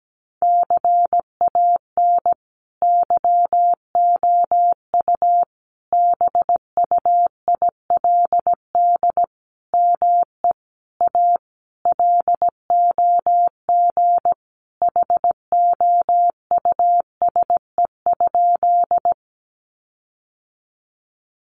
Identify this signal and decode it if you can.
Morse code, heard as a keyed tone, about 17 wpm, 700 Hz. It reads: CAN YOU BUILD ME A LOG HOUSE?
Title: Sentences from Top 1400 Words 17wpm